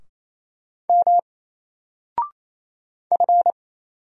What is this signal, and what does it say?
Morse code, heard as a keyed tone, about 27 wpm, 700 Hz. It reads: M F
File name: Single-Letter-Number-28wpm